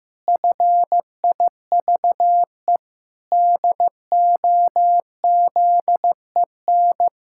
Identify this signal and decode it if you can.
Morse code, heard as a keyed tone, about 15 wpm, 700 Hz. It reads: FIVE DOZEN
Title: Pangram Challenge v2 - 15wpm